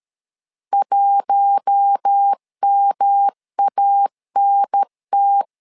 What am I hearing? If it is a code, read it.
Morse code, heard as a keyed tone, about 13 wpm, 800 Hz. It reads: 1MANT
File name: LEZIONI / 4 Q1 12wpm 2028